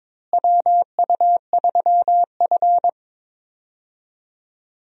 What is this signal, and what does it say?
Morse code, heard as a keyed tone, about 22 wpm, 700 Hz. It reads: WU3F
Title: USA Call Signs - 4 Characters - Repeated 3x - 22wpm